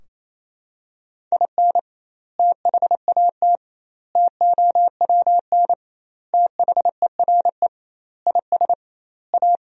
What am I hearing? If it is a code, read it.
Morse code, heard as a keyed tone, about 28 wpm, 700 Hz. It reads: IN THAT TOWN THERE IS A
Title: 2x Word-Spacing - Sentences from Top 500 Words - 28wpm